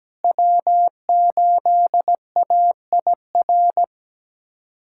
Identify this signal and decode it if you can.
Morse code, heard as a keyed tone, about 17 wpm, 700 Hz. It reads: W8AIR
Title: USA Callsign DE Callsign 17wpm